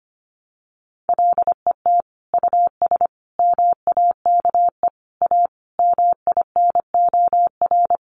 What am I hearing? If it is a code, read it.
Morse code, heard as a keyed tone, about 25 wpm, 700 Hz. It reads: LET US MAKE A MINOR
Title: Sentences from Top 1500 Words 25wpm